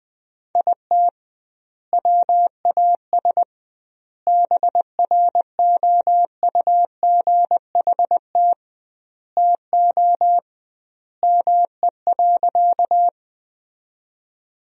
Morse code, heard as a keyed tone, about 20 wpm, 700 Hz. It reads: IT WAS BROUGHT TO ME.
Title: 2x Word-Spacing - Sentences from Top 500 Words - 20wpm